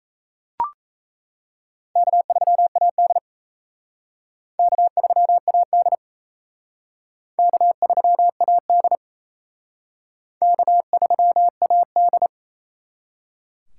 Morse code, heard as a keyed tone, about 35 wpm, 700 Hz. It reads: K3AD K3AD K3AD K3AD
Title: Speed Racing - USA 1x2 Call Signs 28wpm